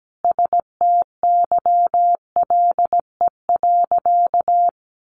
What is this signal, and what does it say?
Morse code, heard as a keyed tone, about 17 wpm, 700 Hz. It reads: STYLE.
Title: Sentences from Top 800 Words 17wpm